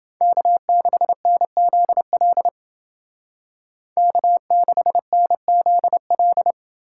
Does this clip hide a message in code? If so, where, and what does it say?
Yes: Morse code from 0.2 to 6.5 s, heard as a keyed tone, about 29 wpm, 700 Hz. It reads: K6NZL K6NZL